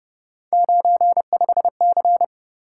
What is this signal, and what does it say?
Morse code, heard as a keyed tone, about 31 wpm, 700 Hz. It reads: 95C
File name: Speed-Racing - Three Letter-Number Combinations - 25wpm